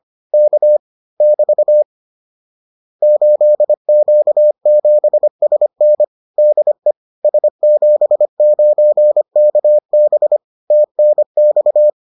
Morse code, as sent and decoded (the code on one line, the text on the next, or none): -.- / -...- / ---.. --.- --... ... -. / -.. / ... --... ----. -.- -... / - -. -..-
K <BT> 8Q7SN DE S79KB TNX